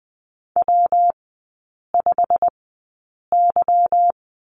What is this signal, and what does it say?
Morse code, heard as a keyed tone, about 20 wpm, spaced as for 8 wpm, 700 Hz. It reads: W5Y
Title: 166 Lesson 27e - Number 7 - Three Characters 20wpm